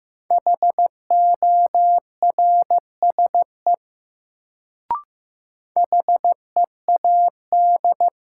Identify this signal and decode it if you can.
Morse code, heard as a keyed tone, about 15 wpm, 700 Hz. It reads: HORSE HEAD
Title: Sentences from Top 300 Words - New Word Drill - 15wpm